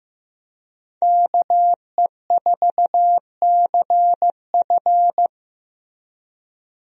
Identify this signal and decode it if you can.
Morse code, heard as a keyed tone, about 16 wpm, 700 Hz. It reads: KE4CF
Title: USA Call Signs - 5 Characters - Repeated 2x - 15wpm